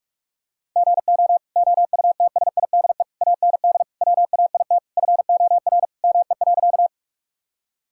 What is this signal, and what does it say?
Morse code, heard as a keyed tone, about 42 wpm, 700 Hz. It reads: GO OUTSIDE AND WAIT FOR ME.
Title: Sentences from Top 1000 Words - Review Part 6 - 45wpm